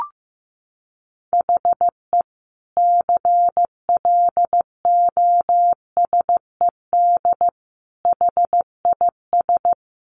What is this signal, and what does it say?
Morse code, heard as a keyed tone, about 15 wpm, 700 Hz. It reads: HE CLOSED HIS
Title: Sentences from Top 1400 Words 15wpm